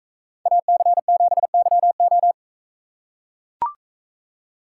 Morse code, heard as a keyed tone, about 42 wpm, 700 Hz. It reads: AC7YO